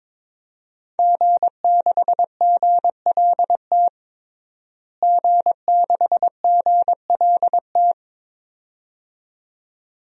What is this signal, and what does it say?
Morse code, heard as a keyed tone, about 22 wpm, 700 Hz. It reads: G6GLT G6GLT
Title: DXCC-100 Call Signs - 5 Characters - Repeated 2x - 22wpm